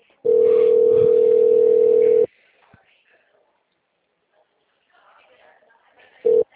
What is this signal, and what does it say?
Ringback tone (North America), 440 + 480 Hz, with speech.
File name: out- -2010- - - .363